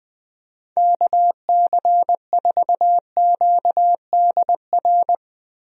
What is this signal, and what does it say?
Morse code, heard as a keyed tone, about 20 wpm, 700 Hz. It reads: KC4QDR
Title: USA Call Signs - 6 Characters - Repeated 2x - 20wpm